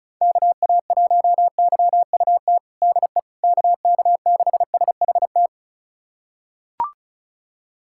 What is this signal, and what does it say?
Morse code, heard as a keyed tone, about 34 wpm, 700 Hz. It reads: KA1YUT DE KK6SHT
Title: USA Callsign DE Callsign 35wpm